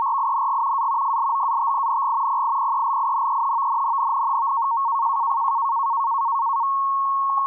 Two-tone teleprinter signal, about 170 Hz shift.
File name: PhosphorTEX / navtex 518khz 8min